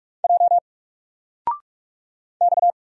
Morse code, heard as a keyed tone, about 44 wpm, 700 Hz. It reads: J X
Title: Single Character (Letter:Number:Punctuation:Prosign) 45wpm